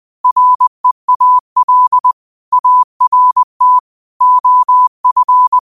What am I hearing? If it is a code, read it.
Morse code, heard as a keyed tone, about 20 wpm, 1000 Hz. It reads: REAL ART OF